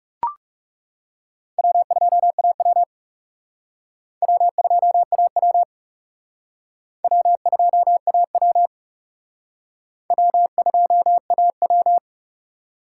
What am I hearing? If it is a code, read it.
Morse code, heard as a keyed tone, about 36 wpm, 700 Hz. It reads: W2AW W2AW W2AW W2AW
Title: Speed Racing - USA 1x2 Call Signs 30wpm